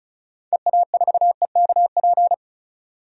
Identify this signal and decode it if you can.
Morse code, heard as a keyed tone, about 34 wpm, 700 Hz. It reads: EA4EKP
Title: DXCC-100 Call Signs - 6 Characters - 35wpm